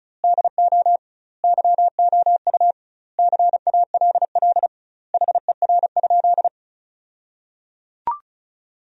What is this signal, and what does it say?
Morse code, heard as a keyed tone, about 36 wpm, 700 Hz. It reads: DO YOU CALL HER?
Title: 2x Word-Spacing - Sentences from Top 100 Words - 35wpm